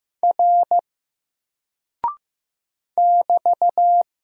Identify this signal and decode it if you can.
Morse code, heard as a keyed tone, about 15 wpm, 700 Hz. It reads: R <BT>
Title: Single Character (Letter:Number:Punctuation:Prosign) Alt 15wpm